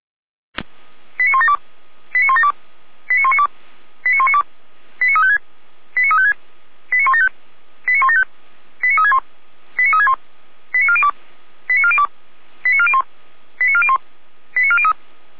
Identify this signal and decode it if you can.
Five-tone selective call (ZVEI), heard as quick runs of five tones, tones of about 70 ms. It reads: 89172 89172 89192 89192 89356 89356 89175 89175 89361 89361 89402 89402 89401 89401 89403